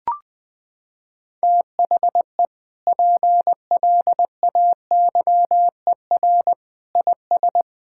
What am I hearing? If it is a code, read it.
Morse code, heard as a keyed tone, about 20 wpm, 700 Hz. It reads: THE PLAYER IS
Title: Sentences with 6-Word Phrases - 20wpm